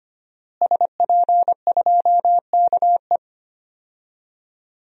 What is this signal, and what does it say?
Morse code, heard as a keyed tone, about 25 wpm, 700 Hz. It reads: SP2KE
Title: DXCC-100 Call Signs - 5 Characters - Repeated 3x - 25wpm